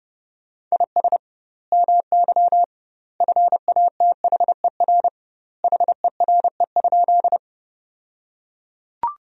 Morse code, heard as a keyed tone, about 30 wpm, 700 Hz. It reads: IS MY FATHER HERE?
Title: 2x Word-Spacing - Sentences from Top 300 Words - 30wpm